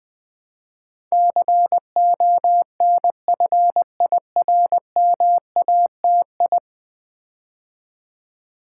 Morse code, heard as a keyed tone, about 20 wpm, 700 Hz. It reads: CONFIRMATI
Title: Build A Word - 5 Letters At A Time - 12-Letter Words - 20wpm